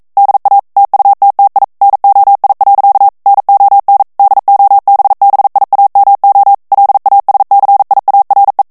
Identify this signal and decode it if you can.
Morse code, heard as a keyed tone, about 42 wpm, 800 Hz. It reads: DA TUTTI NOI. NON DOBBIAMO LASCIARE
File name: morse mp3 / ITALIEN texte clair 1 vitesse 40 mots minute